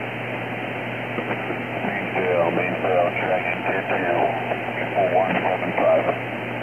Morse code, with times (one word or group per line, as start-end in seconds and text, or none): none